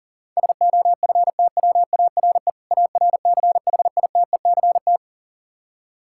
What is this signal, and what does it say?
Morse code, heard as a keyed tone, about 39 wpm, 700 Hz. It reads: SOFTWARE ARCHITECT
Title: Practice QSO - Exchange 3 - More Info - 40wpm